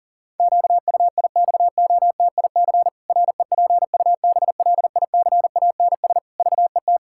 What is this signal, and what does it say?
Morse code, heard as a keyed tone, about 37 wpm, 700 Hz. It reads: QUIXOTIC REPUBLICANS VET
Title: Pangram Challenge v2 - 40wpm